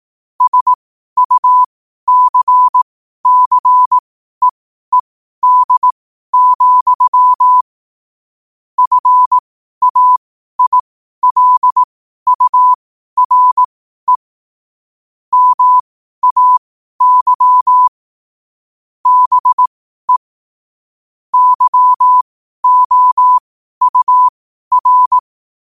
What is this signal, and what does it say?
Morse code, heard as a keyed tone, about 18 wpm, spaced as for 12 wpm, 1000 Hz. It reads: SUCCEED, FAILURE MAY BE YOUR
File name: qotdfiles / 12wpm